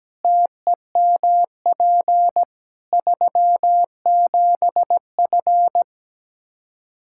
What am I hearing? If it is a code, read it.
Morse code, heard as a keyed tone, about 17 wpm, 700 Hz. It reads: TEMP 37F
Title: Practice QSO - Exchange 2 - Info - 17wpm